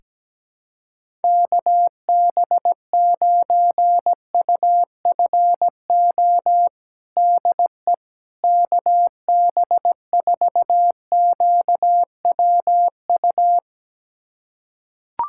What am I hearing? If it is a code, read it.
Morse code, heard as a keyed tone, about 17 wpm, 700 Hz. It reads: KB9UFO DE KB4QWU
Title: USA Callsign DE Callsign - 6 Characters - 17wpm